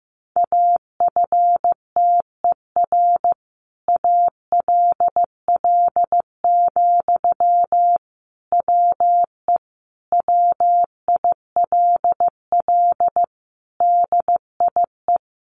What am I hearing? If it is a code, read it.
Morse code, heard as a keyed tone, about 15 wpm, 700 Hz. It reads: AFTER ALL, WE WILL DIE
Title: Sentences from Top 1300 Words 15wpm